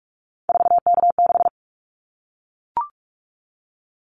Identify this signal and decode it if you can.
Morse code, heard as a keyed tone, about 44 wpm, 700 Hz. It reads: 4K6
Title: Callsign Prefixes for Top 100 DXCC - 45wpm